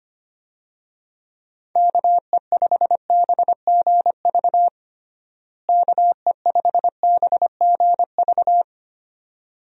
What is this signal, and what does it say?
Morse code, heard as a keyed tone, about 25 wpm, 700 Hz. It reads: KE5BGV KE5BGV
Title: USA Call Signs - 6 Characters - Repeated 2x - 25wpm